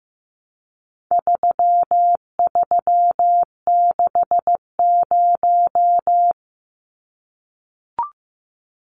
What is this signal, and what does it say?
Morse code, heard as a keyed tone, about 14 wpm, 700 Hz. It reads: 3360
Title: Numbers - 4-Digit 15wpm